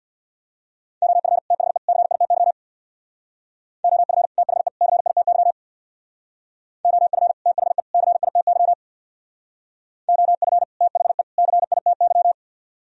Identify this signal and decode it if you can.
Morse code, heard as a keyed tone, about 61 wpm, 700 Hz. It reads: OF THE CITY OF THE CITY OF THE CITY OF THE CITY
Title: Speed-Racing - Sets of 3 Words 50wpm